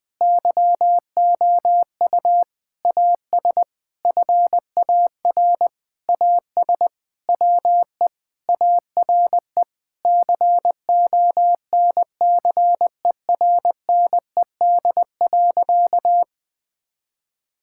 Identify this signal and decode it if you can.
Morse code, heard as a keyed tone, about 20 wpm, 700 Hz. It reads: YOU AS FAR AS WE ARE CONCERNED.
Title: Sentences from Top 1000 Words - Review Part 6 - 20wpm